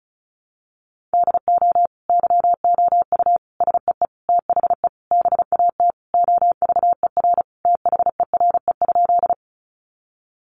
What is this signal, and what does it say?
Morse code, heard as a keyed tone, about 34 wpm, 700 Hz. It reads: DO YOU SEE THE BAT OVER THERE?
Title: Sentences from Top 1500 Words 35wpm